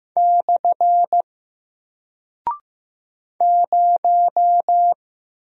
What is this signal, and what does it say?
Morse code, heard as a keyed tone, about 15 wpm, 700 Hz. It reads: / 0
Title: Single Character (Letter:Number:Punctuation:Prosign) Alt 15wpm